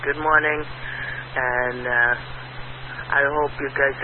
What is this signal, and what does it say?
Emergency Alert System (SAME) tones, heard as harsh buzzing bursts, with speech. It is heard from 0.9 to 3.3 s.